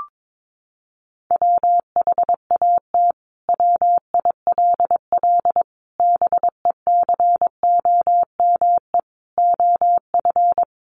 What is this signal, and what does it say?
Morse code, heard as a keyed tone, about 22 wpm, 700 Hz. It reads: WHAT WILL BECOME OF
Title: Sentences from Top 600 Words 22wpm